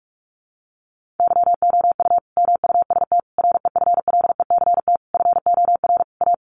Morse code, heard as a keyed tone, about 44 wpm, 700 Hz. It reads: YOU MUST REFLECT FOR A
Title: Sentences from Top 1500 Words 45wpm